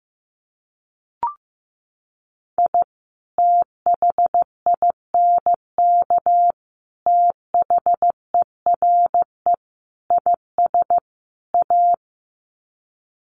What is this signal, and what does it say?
Morse code, heard as a keyed tone, about 15 wpm, 700 Hz. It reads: I THINK THERE IS A
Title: Sets of 5 Words 15wpm